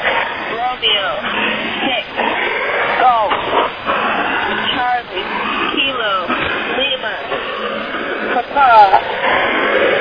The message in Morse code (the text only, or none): none